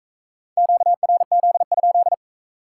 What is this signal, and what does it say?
Morse code, heard as a keyed tone, about 40 wpm, 700 Hz. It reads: QRZ?